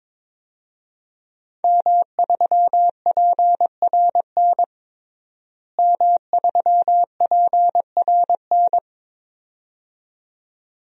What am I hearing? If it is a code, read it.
Morse code, heard as a keyed tone, about 21 wpm, 700 Hz. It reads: M3PRN M3PRN
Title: DXCC-100 Call Signs - 5 Characters - Repeated 2x - 22wpm